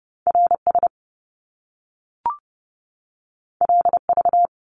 Morse code, heard as a keyed tone, about 30 wpm, 700 Hz. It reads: RS LV
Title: Top Two Letter Combinations in Words 30wpm